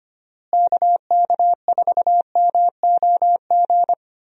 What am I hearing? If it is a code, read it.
Morse code, heard as a keyed tone, about 25 wpm, 700 Hz. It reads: KK4MOG
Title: USA Call Signs - 6 Characters - Repeated 3x - 25wpm